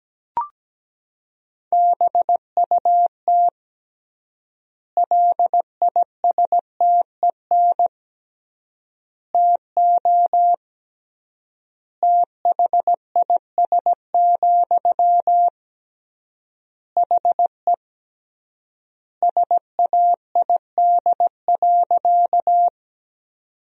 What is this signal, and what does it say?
Morse code, heard as a keyed tone, about 17 wpm, 700 Hz. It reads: BUT LISTEN TO THIS, HE SAID.
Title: 3x Word-Spacing - Sentences from Top 400 Words - 17wpm